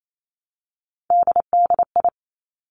Morse code, heard as a keyed tone, about 28 wpm, 690 Hz. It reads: DDI
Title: Three Challenging Letter-Number Combinations 28wpm